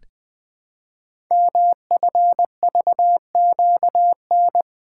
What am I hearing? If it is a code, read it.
Morse code, heard as a keyed tone, about 20 wpm, 700 Hz. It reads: MFVQN
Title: Code Groups - 5 Letters - 20wpm